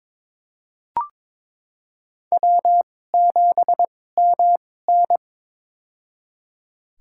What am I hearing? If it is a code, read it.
Morse code, heard as a keyed tone, about 22 wpm, spaced as for 16 wpm, 700 Hz. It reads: W7MN
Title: USA Call Signs - 4 Characters - 2x Character Spacing - 22wpm